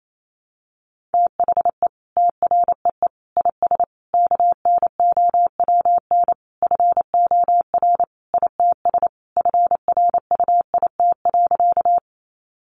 Morse code, heard as a keyed tone, about 28 wpm, 700 Hz. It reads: THE TREE IS KNOWN FOR ITS FRUIT.
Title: Sentences from Top 1300 Words 28wpm